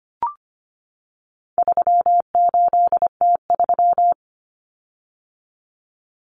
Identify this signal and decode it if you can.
Morse code, heard as a keyed tone, about 25 wpm, 700 Hz. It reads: 38T3
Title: Common Cut Numbers - 4-Digit 25wpm